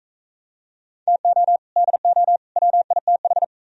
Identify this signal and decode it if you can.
Morse code, heard as a keyed tone, about 39 wpm, 700 Hz. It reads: TO DO WITH